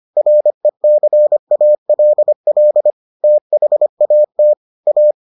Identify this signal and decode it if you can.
Morse code, heard as a keyed tone, about 25 wpm, 600 Hz. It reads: RECALL THAT A